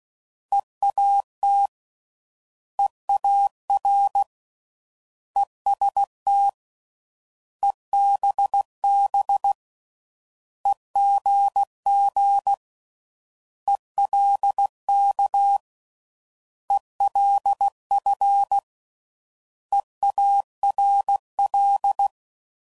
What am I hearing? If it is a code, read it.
Morse code, heard as a keyed tone, about 16 wpm, 800 Hz. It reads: EAT EAR EST EBB EGG ELK ELF EARL